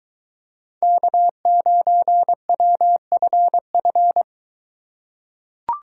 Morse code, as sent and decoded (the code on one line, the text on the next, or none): -.- ----. .-- ..-. ..-.
K9WFF